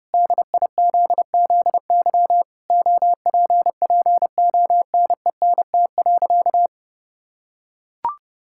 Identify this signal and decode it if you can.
Morse code, heard as a keyed tone, about 30 wpm, 700 Hz. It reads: DIZZY OPPONENT.
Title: Pangram Challenge v2 - 30wpm